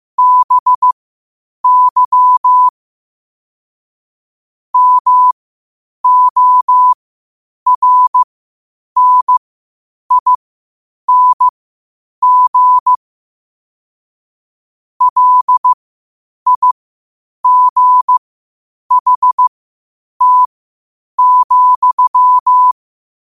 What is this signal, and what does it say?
Morse code, heard as a keyed tone, about 15 wpm, spaced as for 8 wpm, 1000 Hz. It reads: BY MORNING LIGHT,
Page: qotd-at-8-wpm